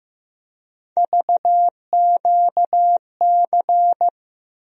Morse code, heard as a keyed tone, about 15 wpm, 700 Hz. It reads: VQC
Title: USA Call Sign Postfixes - 3 Letters 15wpm